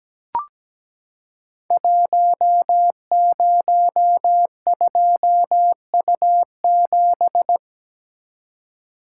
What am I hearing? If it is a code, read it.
Morse code, heard as a keyed tone, about 17 wpm, 700 Hz. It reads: 102U7
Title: Code Groups - 5 Letter:Numbers - 17wpm